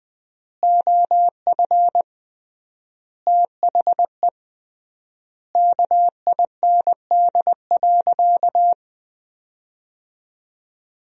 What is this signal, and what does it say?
Morse code, heard as a keyed tone, about 20 wpm, 700 Hz. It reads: OF THE KIND.
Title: 3x Word-Spacing - Sentences from Top 500 Words - 20wpm